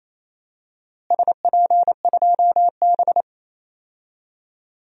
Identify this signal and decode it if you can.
Morse code, heard as a keyed tone, about 28 wpm, 700 Hz. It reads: SP2B
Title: DXCC-100 Call Signs - 4 Characters - Repeated 2x - 28wpm